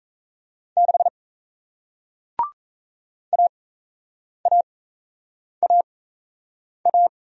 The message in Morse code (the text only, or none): 6 AAAA